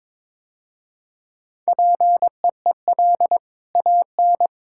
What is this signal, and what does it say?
Morse code, heard as a keyed tone, about 21 wpm, 700 Hz. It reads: PEEL AN
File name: practice / 22wpm / Easy Sentences with Hints - 22wpm